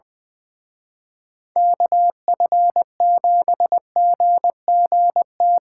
Morse code, heard as a keyed tone, about 24 wpm, 700 Hz. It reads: KF7GGT
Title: Speed Racing - USA 2x3 Call Signs 20wpm